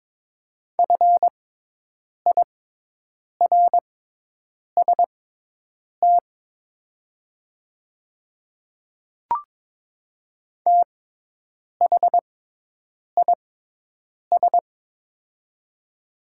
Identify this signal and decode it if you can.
Morse code, heard as a keyed tone, about 22 wpm, spaced as for 8 wpm, 690 Hz. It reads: FIRST THIS
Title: 6x Character Spacing - Top 500 Words - 22wpm